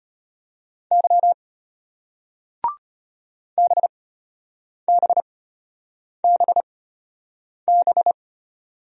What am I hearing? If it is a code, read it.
Morse code, heard as a keyed tone, about 32 wpm, spaced as for 8 wpm, 700 Hz. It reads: Y BBBB